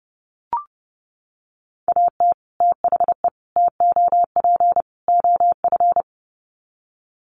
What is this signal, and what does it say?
Morse code, heard as a keyed tone, about 30 wpm, 700 Hz. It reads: AT THE TOP OF
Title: Sets of 4 Words 30wpm